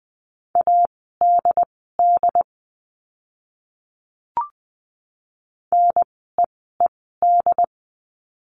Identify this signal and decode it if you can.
Morse code, heard as a keyed tone, about 20 wpm, spaced as for 14 wpm, 700 Hz. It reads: ADD NEED
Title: 101 Lesson 17f - Letter u - Words 20wpm